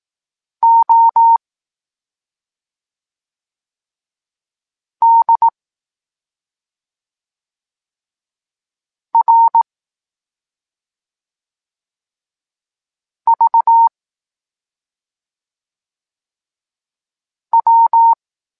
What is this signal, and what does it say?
Morse code, heard as a keyed tone, about 18 wpm, 920 Hz. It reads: O D R V W
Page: Tekst 23.3 – Ciąg dalszy doskonalenia odbioru poznanych liter